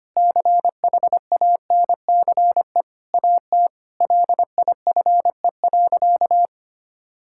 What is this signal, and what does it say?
Morse code, heard as a keyed tone, about 25 wpm, 700 Hz. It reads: CHANCE AT LIFE.
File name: practice / 25wpm / Difficult Sentences with Hints - 25wpm